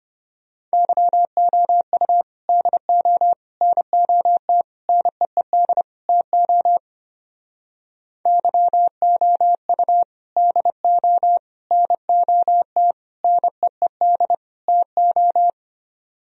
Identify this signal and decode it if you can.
Morse code, heard as a keyed tone, about 28 wpm, 700 Hz. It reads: YOU DO NOT NEED TO YOU DO NOT NEED TO
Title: Speed-Racing - Sets of 5 Words 25wpm